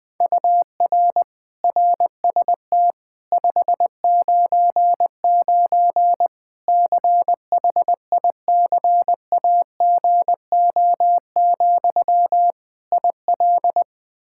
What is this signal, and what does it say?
Morse code, heard as a keyed tone, about 20 wpm, 700 Hz. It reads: UR RST 599 CHICAGO, IL
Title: Practice QSO - Exchange 1 - Beginning - 20wpm